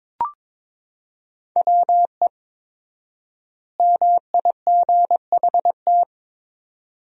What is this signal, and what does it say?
Morse code, heard as a keyed tone, about 21 wpm, 700 Hz. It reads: WE MIGHT
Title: 4x Word-Spacing - Sentences from Top 300 Words - 22wpm